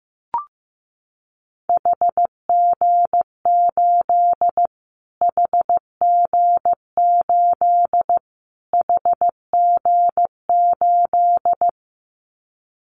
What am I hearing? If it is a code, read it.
Morse code, heard as a keyed tone, about 15 wpm, 700 Hz. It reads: HG8 HG8 HG8
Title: Callsign Prefixes for Top 25 DXCC - Repeated 3x - 15wpm